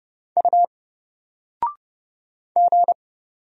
Morse code, heard as a keyed tone, about 31 wpm, 700 Hz. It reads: U G
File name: Single-Letter-Number-30wpm